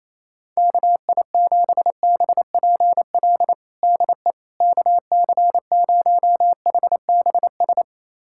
Morse code, heard as a keyed tone, about 28 wpm, 700 Hz. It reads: KI7BPL DE KC0HBS
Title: USA Callsign DE Callsign - 6 Characters - 28wpm